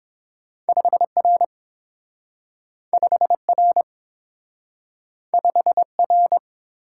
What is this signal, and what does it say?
Morse code, heard as a keyed tone, about 30 wpm, 700 Hz. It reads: 5R 5R 5R